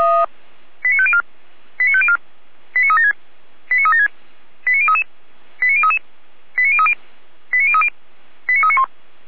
Five-tone selective call (ZVEI), heard as quick runs of five tones, tones of about 70 ms. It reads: 89504 89504 89376 89376 89033 89033 89030 89030 89391